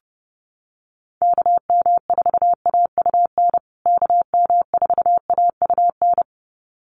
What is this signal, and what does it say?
Morse code, heard as a keyed tone, about 30 wpm, 700 Hz. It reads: KM4AUN KM4AUN
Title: USA Call Signs Repeated 2x 30wpm